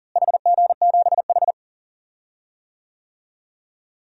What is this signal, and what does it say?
Morse code, heard as a keyed tone, about 39 wpm, 700 Hz. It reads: HG7H
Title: DXCC-100 Call Signs - 4 Characters - Repeated 3x - 40wpm